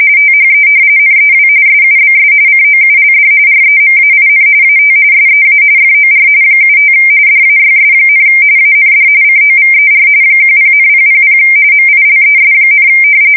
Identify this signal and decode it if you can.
RTTY, about 45.45 baud. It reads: THE NEW SUBS COULD REMAIN AT SEA FOR 75 DAYS, AND COVER 10,000 MILES,
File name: fsk170 / usnavy4